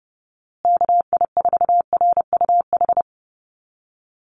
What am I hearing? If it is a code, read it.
Morse code, heard as a keyed tone, about 30 wpm, 700 Hz. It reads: KI4RUH
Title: USA Call Signs Repeated 2x 30wpm